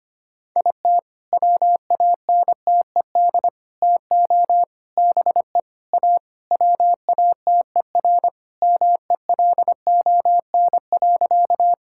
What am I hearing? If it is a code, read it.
Morse code, heard as a keyed tone, about 25 wpm, 700 Hz. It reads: IT WANTED TO BE A WATER MELON.
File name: practice / 25wpm / All-Age Appropriate Jokes - 25wpm